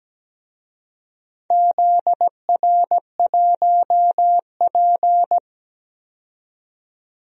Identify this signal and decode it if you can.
Morse code, heard as a keyed tone, about 17 wpm, 700 Hz. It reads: ZR1P
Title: DXCC-100 Call Signs - 4 Characters - 17wpm